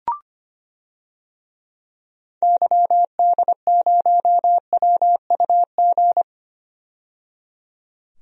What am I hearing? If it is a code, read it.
Morse code, heard as a keyed tone, about 25 wpm, 700 Hz. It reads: YD0WUG
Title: DXCC-100 Call Signs - 6 Characters - 25wpm